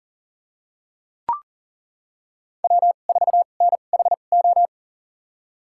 Morse code, heard as a keyed tone, about 40 wpm, spaced as for 29 wpm, 700 Hz. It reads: W4NHO
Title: USA Call Signs - 5 Characters - 2x Character Spacing - 40wpm